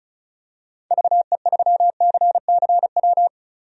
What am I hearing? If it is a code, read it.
Morse code, heard as a keyed tone, about 36 wpm, 700 Hz. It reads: VE3CCW
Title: DXCC-100 Call Signs - 6 Characters - Repeated 3x - 35wpm